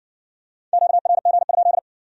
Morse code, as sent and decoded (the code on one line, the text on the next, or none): --.- .-. --. ..--..
QRG?